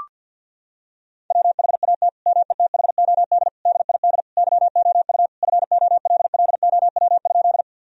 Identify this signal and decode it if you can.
Morse code, heard as a keyed tone, about 48 wpm, 700 Hz. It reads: WHAT METHOD DID YOU FOLLOW?
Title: Sentences from Top 1000 Words - Review Part 3 - 50wpm